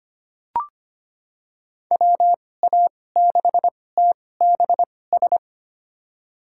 Morse code, heard as a keyed tone, about 25 wpm, spaced as for 18 wpm, 700 Hz. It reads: WA6TBS
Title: USA Call Signs - 2x Character Spacing - 25wpm